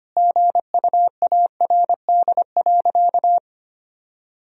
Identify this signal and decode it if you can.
Morse code, heard as a keyed tone, about 25 wpm, 700 Hz. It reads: GUARD.
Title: Sentences from Top 1000 Words - Review Part 4 - 25wpm